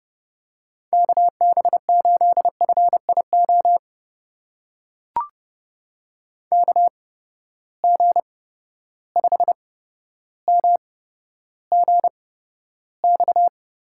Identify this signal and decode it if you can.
Morse code, heard as a keyed tone, about 30 wpm, spaced as for 13 wpm, 700 Hz. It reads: KB8FIO KG5MGX
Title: USA Call Signs - 6 Characters - 8x Character Spacing - 30wpm